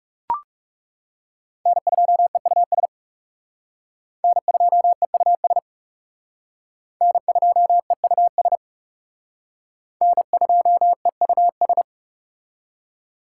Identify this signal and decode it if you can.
Morse code, heard as a keyed tone, about 37 wpm, 700 Hz. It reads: N2EUS N2EUS N2EUS N2EUS
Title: Speed Racing - USA 1x3 Call Signs 30wpm